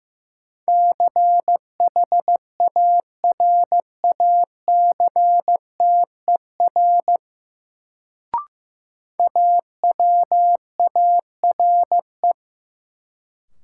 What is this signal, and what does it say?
Morse code, heard as a keyed tone, about 15 wpm, 700 Hz. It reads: CHARACTER AWARE
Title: Sentences from Top 1200 Words - New Word Drill - 15wpm